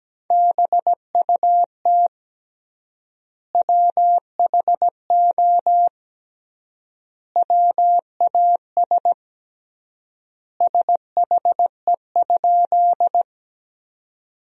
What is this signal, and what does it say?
Morse code, heard as a keyed tone, about 17 wpm, 700 Hz. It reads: BUT WHO WAS SHE?
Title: 3x Word-Spacing - Sentences from Top 100 Words - 17wpm